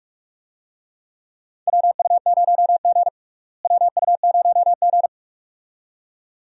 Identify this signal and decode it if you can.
Morse code, heard as a keyed tone, about 46 wpm, 700 Hz. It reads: WU0G WU0G
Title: USA Call Signs - 4 Characters - Repeated 2x - 45wpm